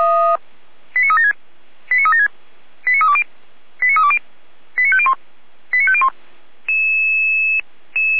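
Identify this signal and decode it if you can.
Five-tone selective call (ZVEI), heard as quick runs of five tones, tones of about 70 ms. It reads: 89376 89376 89320 89320 89501 89501
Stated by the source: ZVEI five-tone call received via SDR.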